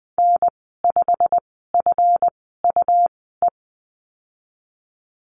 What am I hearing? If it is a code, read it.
Morse code, heard as a keyed tone, about 19 wpm, spaced as for 14 wpm, 700 Hz. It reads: N5FUE
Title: 147 Lesson 24g - Letter y - Call Signs 20wpm